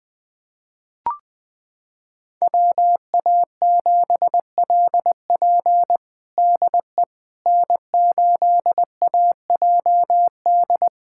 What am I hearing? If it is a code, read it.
Morse code, heard as a keyed tone, about 20 wpm, 700 Hz. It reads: WA7LP DE N8AJD
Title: USA Callsign DE Callsign - 5 Characters - 20wpm